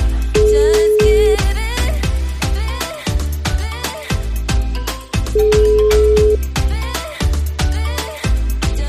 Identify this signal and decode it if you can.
Ringback tone (Europe, 425 Hz).